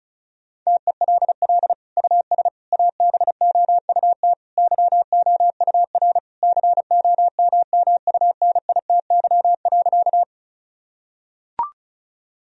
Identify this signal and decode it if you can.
Morse code, heard as a keyed tone, about 36 wpm, 700 Hz. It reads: TELL US ABOUT YOUR COMMUNITY.
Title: Sentences from Top 1000 Words - Review Part 4 - 35wpm